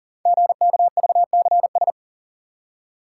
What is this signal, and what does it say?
Morse code, heard as a keyed tone, about 40 wpm, 700 Hz. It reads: GKVCS
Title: Code Groups - 5 Letters - 40wpm